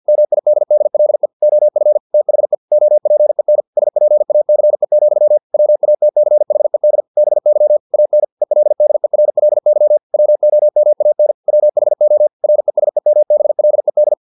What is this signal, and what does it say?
Morse code, heard as a keyed tone, about 50 wpm, 600 Hz. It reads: MIDDLE OF THE OPEN SPACE, WATCHED BY AN ELDERLY WOMAN WHO RESEMBLED
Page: Never 49wpm